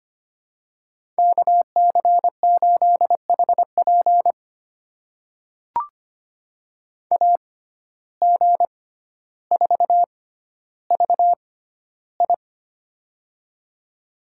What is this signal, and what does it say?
Morse code, heard as a keyed tone, about 25 wpm, spaced as for 13 wpm, 700 Hz. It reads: KC8HP AG4VI
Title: USA Call Signs - 5 Characters - 6x Character Spacing - 25wpm